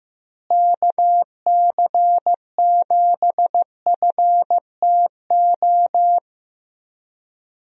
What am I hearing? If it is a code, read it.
Morse code, heard as a keyed tone, about 15 wpm, 700 Hz. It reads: KC7FTO
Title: USA Call Signs - 6 Characters - Repeated 2x - 15wpm